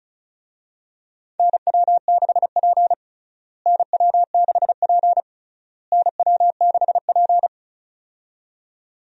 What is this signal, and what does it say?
Morse code, heard as a keyed tone, about 36 wpm, 700 Hz. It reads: NW6P NW6P NW6P
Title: USA Call Signs - 4 Characters - Repeated 3x - 35wpm